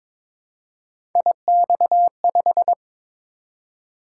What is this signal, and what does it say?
Morse code, heard as a keyed tone, about 23 wpm, 700 Hz. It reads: IX5
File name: practice / 22wpm / Three Letter-Number Combinations - 22wpm